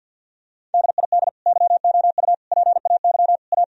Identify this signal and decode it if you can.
Morse code, heard as a keyed tone, about 50 wpm, 700 Hz. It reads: DID YOU PAY A
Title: Sentences from Top 1000 Words - Review Part 4 - 50wpm